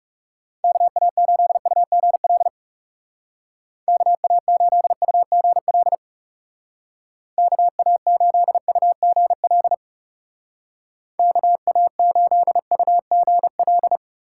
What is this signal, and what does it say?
Morse code, heard as a keyed tone, about 37 wpm, 700 Hz. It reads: KA8UGL KA8UGL KA8UGL KA8UGL
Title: Speed Racing - USA Call Signs 30wpm